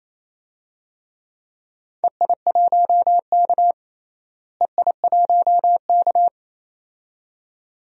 Morse code, heard as a keyed tone, about 28 wpm, 700 Hz. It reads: EI1K EI1K
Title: DXCC-100 Call Signs - 4 Characters - Repeated 2x - 28wpm